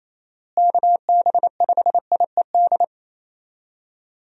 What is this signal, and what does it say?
Morse code, heard as a keyed tone, about 28 wpm, 700 Hz. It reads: KB5IED
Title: USA Call Signs - 6 Characters - Repeated 2x - 28wpm